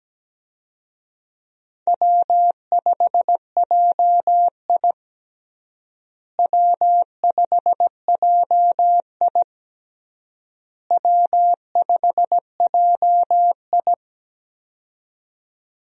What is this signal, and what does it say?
Morse code, heard as a keyed tone, about 17 wpm, 700 Hz. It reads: W5JI W5JI W5JI
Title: USA Call Signs - 4 Characters - Repeated 3x - 17wpm